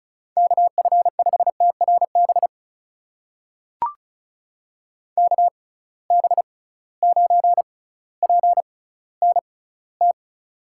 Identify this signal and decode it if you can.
Morse code, heard as a keyed tone, about 34 wpm, spaced as for 18 wpm, 700 Hz. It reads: KF5TRB KB9PNT
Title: USA Call Signs - 6 Characters - 6x Character Spacing - 35wpm